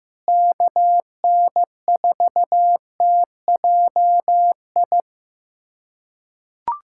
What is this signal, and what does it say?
Morse code, heard as a keyed tone, about 15 wpm, 700 Hz. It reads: KN4TJI
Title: USA Callsign DE Callsign 15wpm